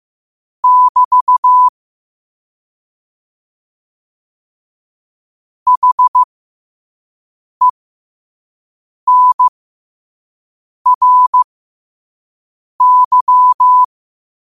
Morse code, heard as a keyed tone, about 15 wpm, spaced as for 5 wpm, 1000 Hz. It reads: <BT> HENRY
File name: qotdfiles / 5wpm